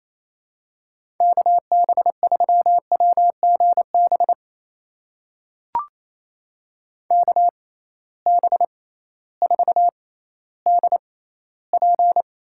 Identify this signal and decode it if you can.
Morse code, heard as a keyed tone, about 28 wpm, spaced as for 15 wpm, 700 Hz. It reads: KB3WGB KB4DP
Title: USA Call Signs - 6 Characters - 6x Character Spacing - 28wpm